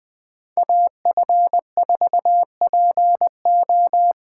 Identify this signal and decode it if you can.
Morse code, heard as a keyed tone, about 20 wpm, 700 Hz. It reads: AF4PO